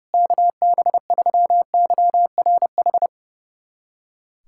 Morse code, heard as a keyed tone, about 32 wpm, 700 Hz. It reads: KB3YRH
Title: USA Callsign DE Callsign 30wpm